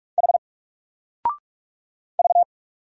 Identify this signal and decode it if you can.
Morse code, heard as a keyed tone, about 45 wpm, 700 Hz. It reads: H V
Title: Single Letter-Number Alt 45wpm